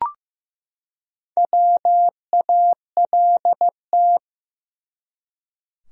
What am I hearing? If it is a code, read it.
Morse code, heard as a keyed tone, about 15 wpm, 700 Hz. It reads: WALT
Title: USA Common Names - Max 4 Letters - 15wpm